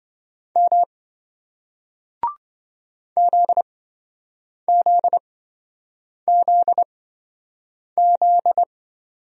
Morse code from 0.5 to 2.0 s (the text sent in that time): M